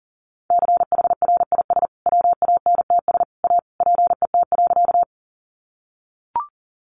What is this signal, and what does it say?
Morse code, heard as a keyed tone, about 40 wpm, 700 Hz. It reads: CHRIS WANTS A PET.
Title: Easy Phrases 40wpm